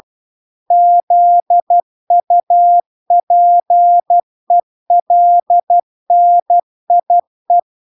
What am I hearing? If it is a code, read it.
Morse code, heard as a keyed tone, about 12 wpm, 700 Hz. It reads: ZUPELNIE